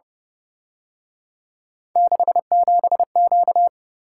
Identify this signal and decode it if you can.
Morse code, heard as a keyed tone, about 29 wpm, 700 Hz. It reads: 67Q
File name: Speed-Racing - Three Letter-Number Combinations - 25wpm